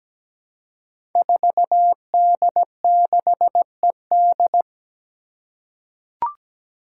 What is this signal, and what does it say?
Morse code, heard as a keyed tone, about 17 wpm, 700 Hz. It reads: 4D6ED
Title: Five Challenging Letter-Number Combinations 17wpm